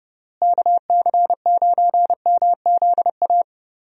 Morse code, heard as a keyed tone, about 30 wpm, 700 Hz. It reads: KC9MZA
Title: USA Call Signs - 6 Characters - Repeated 3x - 30wpm